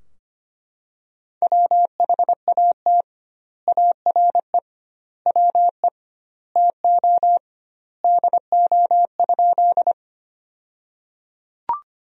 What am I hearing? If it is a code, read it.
Morse code, heard as a keyed tone, about 25 wpm, 700 Hz. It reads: WHAT ARE WE TO DO?
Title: 2x Word-Spacing - Sentences from Top 100 Words - 25wpm